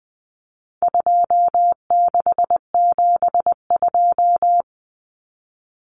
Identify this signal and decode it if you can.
Morse code, heard as a keyed tone, about 20 wpm, 700 Hz. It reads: 2672
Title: Numbers - 4-Digit 20wpm